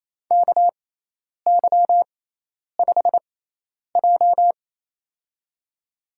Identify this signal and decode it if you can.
Morse code, heard as a keyed tone, about 27 wpm, spaced as for 10 wpm, 700 Hz. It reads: KY5J
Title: USA Call Signs - 4 Characters - 6x Character Spacing - 28wpm